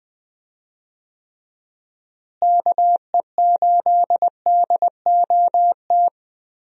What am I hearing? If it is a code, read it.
Morse code, heard as a keyed tone, about 20 wpm, 700 Hz. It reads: KE8DOT
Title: USA Call Signs - 6 Characters - Repeated 3x - 20wpm